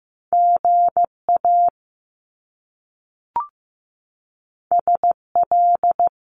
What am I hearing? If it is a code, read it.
Morse code, heard as a keyed tone, about 15 wpm, 700 Hz. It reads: GA SL
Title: Top Two Letter Combinations in Words 15wpm